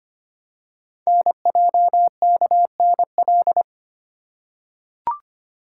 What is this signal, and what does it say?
Morse code, heard as a keyed tone, about 25 wpm, 700 Hz. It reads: NJKNL
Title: Code Groups - 5 Letter:Numbers - 25wpm